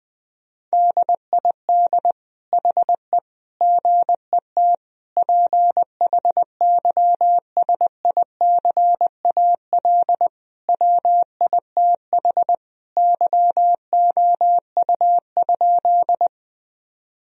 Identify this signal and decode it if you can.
Morse code, heard as a keyed tone, about 21 wpm, 700 Hz. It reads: DID HE GET PHYSICAL WITH YOU?
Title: Sentences from Top 1000 Words - Review Part 4 - 20wpm